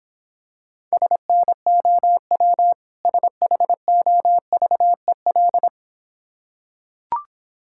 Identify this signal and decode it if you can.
Morse code, heard as a keyed tone, about 26 wpm, 700 Hz. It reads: SNOW SHOVEL